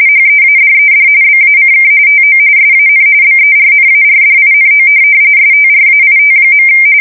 RTTY, about 45.45 baud. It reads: HEADED BACK TO SAN BERNARDINO STRAIT.
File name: fsk170 / USS-New-Jersey